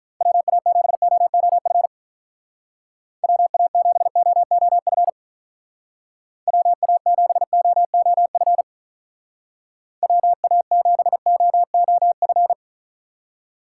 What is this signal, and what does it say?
Morse code, heard as a keyed tone, about 43 wpm, 700 Hz. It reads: WA7OOF WA7OOF WA7OOF WA7OOF
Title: Speed Racing - USA Call Signs 35wpm